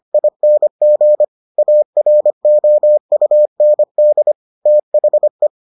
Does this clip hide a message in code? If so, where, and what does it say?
Yes: Morse code from 0.1 to 5.5 s, heard as a keyed tone, about 25 wpm, 600 Hz. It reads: ING AROUND THE